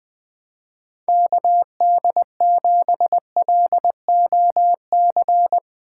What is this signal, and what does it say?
Morse code, heard as a keyed tone, about 20 wpm, 700 Hz. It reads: KD7LOC
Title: USA Call Signs - 6 Characters - 2x Character Spacing - 20wpm